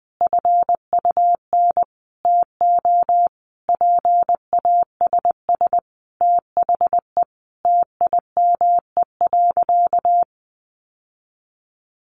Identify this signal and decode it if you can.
Morse code, heard as a keyed tone, about 20 wpm, 700 Hz. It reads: FUN TO PASS THE TIME.
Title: Sentences from Top 700 Words 20wpm